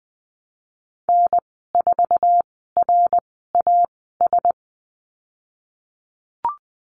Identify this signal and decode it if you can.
Morse code, heard as a keyed tone, about 20 wpm, spaced as for 15 wpm, 700 Hz. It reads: N4RAS
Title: 055 Lesson 10g - Letter r - Call Signs 20wpm